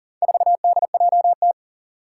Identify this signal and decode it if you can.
Morse code, heard as a keyed tone, about 40 wpm, 700 Hz. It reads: 4DJT
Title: Four Challenging Letter-Number Combinations 40wpm